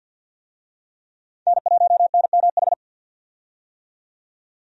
Morse code, heard as a keyed tone, about 52 wpm, 700 Hz. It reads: N1NMH
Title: USA Call Signs - 5 Characters - 50wpm